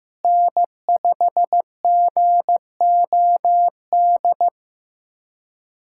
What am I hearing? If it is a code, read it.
Morse code, heard as a keyed tone, about 15 wpm, 700 Hz. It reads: N5GOD
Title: USA Callsign DE Callsign - 5 Characters - 15wpm